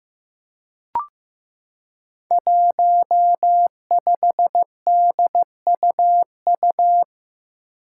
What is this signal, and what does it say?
Morse code, heard as a keyed tone, about 15 wpm, 700 Hz. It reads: 15DUU
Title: Five Challenging Letter-Number Combinations 15wpm